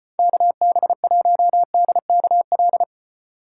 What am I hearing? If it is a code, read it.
Morse code, heard as a keyed tone, about 34 wpm, 700 Hz. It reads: KB1DKL